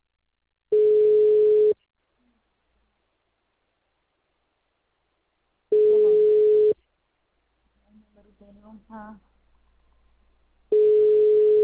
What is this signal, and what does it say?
Ringback tone (Europe, 425 Hz), with speech.
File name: out- -2002- - - .7